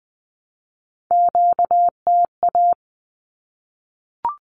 Morse code, heard as a keyed tone, about 19 wpm, 700 Hz. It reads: QTA
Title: Q-Codes 20wpm